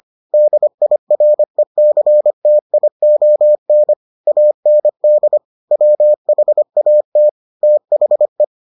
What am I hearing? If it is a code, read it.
Morse code, heard as a keyed tone, about 25 wpm, 600 Hz. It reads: DIRECTION AND WHAT THE